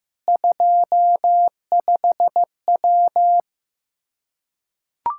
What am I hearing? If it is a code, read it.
Morse code, heard as a keyed tone, about 15 wpm, 700 Hz. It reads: 25W
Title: Practice QSO - Exchange 2 - Info - 15wpm